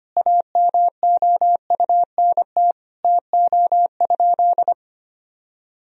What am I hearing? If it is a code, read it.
Morse code, heard as a keyed tone, about 25 wpm, 700 Hz. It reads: AMOUNT TO?
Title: Sentences from Top 1000 Words - Review Part 4 - 25wpm